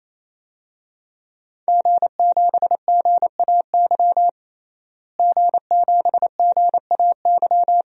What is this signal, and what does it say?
Morse code, heard as a keyed tone, about 28 wpm, 700 Hz. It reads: G7GAY G7GAY
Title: DXCC-100 Call Signs - 5 Characters - Repeated 3x - 28wpm